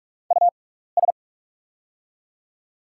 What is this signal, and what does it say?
Morse code, heard as a keyed tone, about 48 wpm, spaced as for 16 wpm, 700 Hz. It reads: US
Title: 6x Character Spacing - Top 200 Words - 45wpm